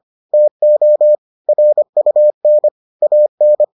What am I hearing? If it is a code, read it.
Morse code, heard as a keyed tone, about 25 wpm, 600 Hz. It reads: TO RUN AN